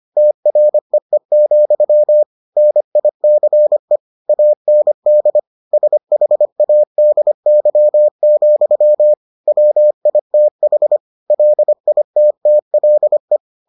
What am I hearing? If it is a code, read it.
Morse code, heard as a keyed tone, about 25 wpm, 600 Hz. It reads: TREE, NICE AND SHADY, WITH LITTLE